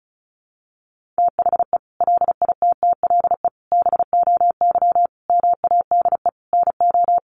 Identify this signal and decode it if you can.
Morse code, heard as a keyed tone, about 34 wpm, 700 Hz. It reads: THE LITTLE BOY MADE NO
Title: Sentences from Top 1400 Words 35wpm